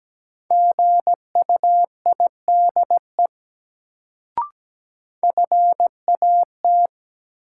Morse code, heard as a keyed tone, about 17 wpm, 700 Hz. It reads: GUIDE FAT
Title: Sentences from Top 600 Words - New Word Drill - 17wpm